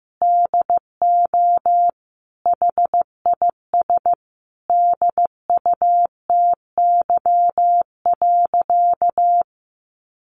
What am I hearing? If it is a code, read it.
Morse code, heard as a keyed tone, about 15 wpm, 700 Hz. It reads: DO HIS DUTY.
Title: Sentences from Top 1500 Words 15wpm